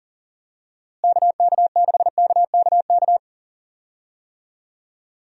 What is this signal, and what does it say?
Morse code, heard as a keyed tone, about 40 wpm, 700 Hz. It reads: KK6KKK
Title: USA Call Signs - 6 Characters - Repeated 3x - 40wpm